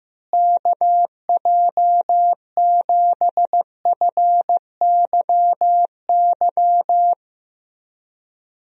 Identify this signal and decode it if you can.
Morse code, heard as a keyed tone, about 15 wpm, 700 Hz. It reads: KJ7FYY
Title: USA Call Signs - 6 Characters - Repeated 3x - 15wpm